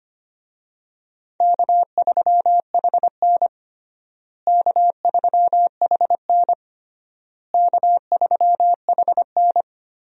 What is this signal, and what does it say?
Morse code, heard as a keyed tone, about 25 wpm, 700 Hz. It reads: K3HN K3HN K3HN
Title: USA Call Signs - 4 Characters - Repeated 3x - 25wpm